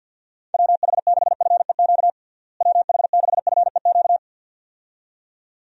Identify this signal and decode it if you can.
Morse code, heard as a keyed tone, about 51 wpm, 700 Hz. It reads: WH6FEQ WH6FEQ
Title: USA Call Signs - 6 Characters - Repeated 2x - 50wpm